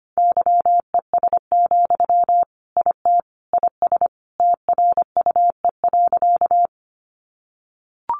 Morse code, heard as a keyed tone, about 25 wpm, 700 Hz. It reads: YES, IT IS TRUE.
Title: Sentences from Top 500 Words 25wpm